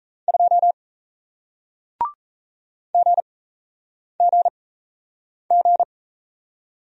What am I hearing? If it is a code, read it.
Morse code, heard as a keyed tone, about 37 wpm, 700 Hz. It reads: 2 G G G